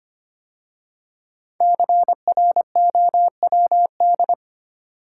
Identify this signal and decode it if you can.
Morse code, heard as a keyed tone, about 25 wpm, 700 Hz. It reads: CROWD
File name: 5 Letter Words - First 3 Letters Given - 25wpm